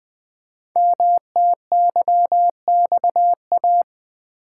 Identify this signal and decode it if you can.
Morse code, heard as a keyed tone, about 20 wpm, 700 Hz. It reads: MTYXA
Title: Code Groups - 5 Letters - 20wpm